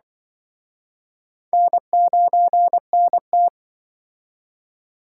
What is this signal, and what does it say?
Morse code, heard as a keyed tone, about 23 wpm, 700 Hz. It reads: N9NT
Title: Speed Racing - USA 1x2 Call Signs 20wpm